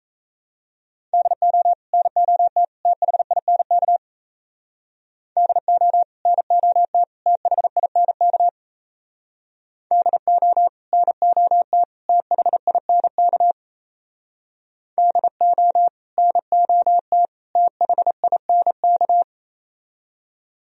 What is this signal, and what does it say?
Morse code, heard as a keyed tone, about 36 wpm, 700 Hz. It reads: DO NOT THINK DO NOT THINK DO NOT THINK DO NOT THINK